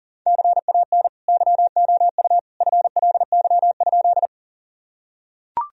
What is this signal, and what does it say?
Morse code, heard as a keyed tone, about 40 wpm, 700 Hz. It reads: CAN YOU FLY?
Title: Sentences from Top 1000 Words - Review Part 2 - 40wpm